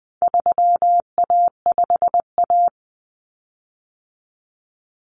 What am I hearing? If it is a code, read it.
Morse code, heard as a keyed tone, about 20 wpm, 700 Hz. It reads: 3A5A
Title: Common Cut Numbers - 4-Digit 20wpm